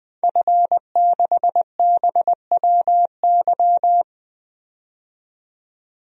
Morse code, heard as a keyed tone, about 20 wpm, 700 Hz. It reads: F6BWY
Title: DXCC-100 Call Signs - 5 Characters - 20wpm